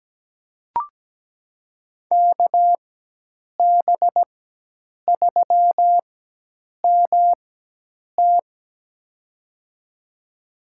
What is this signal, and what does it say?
Morse code, heard as a keyed tone, about 17 wpm, spaced as for 8 wpm, 700 Hz. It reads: KB3MT
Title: USA Call Signs - 4x Character Spacing - 17wpm